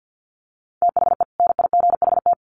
Morse code, heard as a keyed tone, about 48 wpm, 700 Hz. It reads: THE NIGHT
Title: Sentences from Top 1300 Words 50wpm